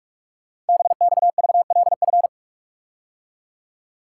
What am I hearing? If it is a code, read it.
Morse code, heard as a keyed tone, about 44 wpm, 700 Hz. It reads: BXVLF
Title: Code Groups - 5 Letters - 45wpm